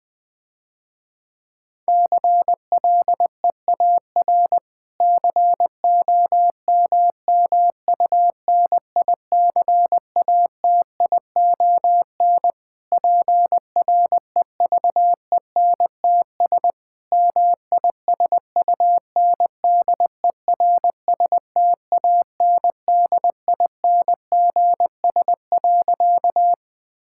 Morse code, heard as a keyed tone, about 20 wpm, 700 Hz. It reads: CLEAR COMMUNICATION PREVENTS MISUNDERSTANDINGS.
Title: Difficult Sentences with Hints - 20wpm